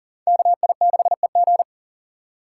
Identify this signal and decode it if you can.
Morse code, heard as a keyed tone, about 40 wpm, 700 Hz. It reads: KI6EG
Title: USA Call Signs - 5 Characters - 4x Character Spacing - 40wpm